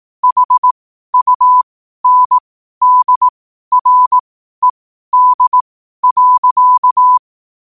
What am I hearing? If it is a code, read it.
Morse code, heard as a keyed tone, about 18 wpm, spaced as for 12 wpm, 1000 Hz. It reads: HUNDRED.
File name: qotdfiles / 12wpm